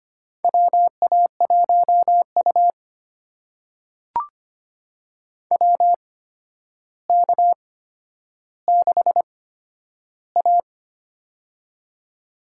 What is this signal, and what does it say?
Morse code, heard as a keyed tone, about 25 wpm, spaced as for 11 wpm, 700 Hz. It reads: WA1U WK6A
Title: USA Call Signs - 4 Characters - 8x Character Spacing - 25wpm